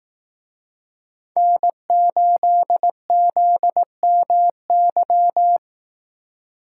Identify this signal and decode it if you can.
Morse code, heard as a keyed tone, about 18 wpm, 700 Hz. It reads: N8ZMY
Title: Speed Racing - USA Call Signs 15wpm